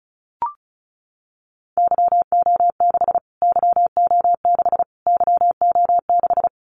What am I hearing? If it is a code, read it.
Morse code, heard as a keyed tone, about 34 wpm, 700 Hz. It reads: YO6 YO6 YO6
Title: Callsign Prefixes for Top 25 DXCC - Repeated 3x - 35wpm